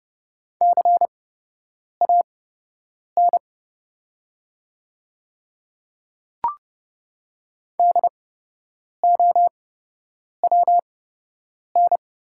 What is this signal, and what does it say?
Morse code, heard as a keyed tone, about 30 wpm, spaced as for 8 wpm, 700 Hz. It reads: CAN DOWN
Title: 8x Character Spacing - Top 200 Words - 30wpm